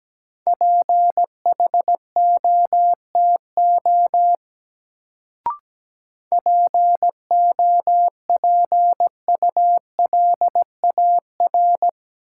Morse code, heard as a keyed tone, about 17 wpm, 700 Hz. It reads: PHOTO POPULAR
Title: Sentences from Top 1200 Words - New Word Drill Alt - 17wpm